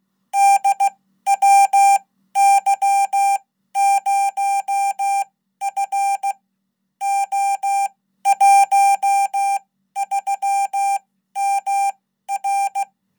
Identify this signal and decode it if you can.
Morse code, heard as a keyed tone, about 16 wpm, spaced as for 12 wpm, 790 Hz. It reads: DWY0F O13MR
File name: Code